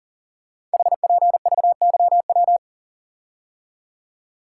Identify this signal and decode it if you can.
Morse code, heard as a keyed tone, about 42 wpm, 690 Hz. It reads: HPVYW
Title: Code Groups - 5 Letters - 40wpm